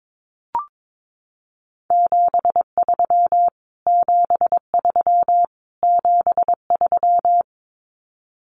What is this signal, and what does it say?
Morse code, heard as a keyed tone, about 22 wpm, 700 Hz. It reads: 73 73 73
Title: Abbreviations Repeated 3x - 22wpm